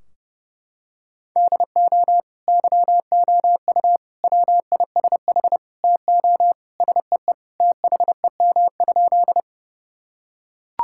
Morse code, heard as a keyed tone, about 31 wpm, 700 Hz. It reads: DO YOU WISH TO SEE THEM?
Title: Sentences from Top 1600 Words 30wpm